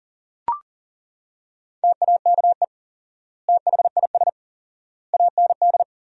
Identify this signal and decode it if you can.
Morse code, heard as a keyed tone, about 40 wpm, 700 Hz. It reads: TAKE THIS AND
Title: 4x Word-Spacing - Sentences from Top 200 Words - 40wpm